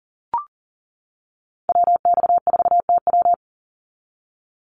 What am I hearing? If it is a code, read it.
Morse code, heard as a keyed tone, about 42 wpm, 700 Hz. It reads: WX4TW
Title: NA SOTA Activator Callsigns 40wpm